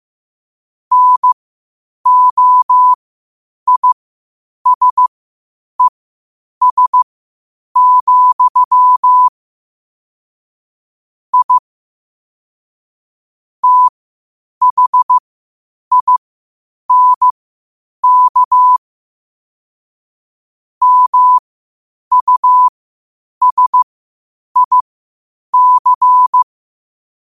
Morse code, heard as a keyed tone, about 15 wpm, spaced as for 8 wpm, 1000 Hz. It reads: NOISES, I THINK MUSIC
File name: qotdfiles / 8wpm